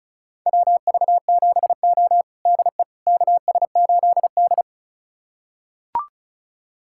Morse code, heard as a keyed tone, about 34 wpm, 700 Hz. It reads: WV7O DE KS8D
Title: USA Callsign DE Callsign - 4 Characters - 35wpm